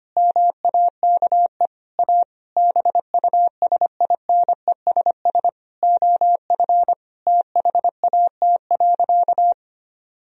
Morse code, heard as a keyed tone, about 25 wpm, 700 Hz. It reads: MAKE A BUSINESS OF THAT.
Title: Sentences from Top 1000 Words - Review Part 4 - 25wpm